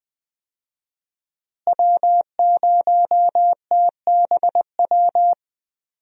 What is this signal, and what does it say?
Morse code, heard as a keyed tone, about 20 wpm, 700 Hz. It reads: W0TBW
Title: USA Call Signs - 5 Characters - Repeated 3x - 20wpm